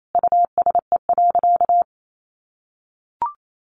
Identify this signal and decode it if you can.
Morse code, heard as a keyed tone, about 28 wpm, 700 Hz. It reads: USE.
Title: Sentences from Top 1400 Words 28wpm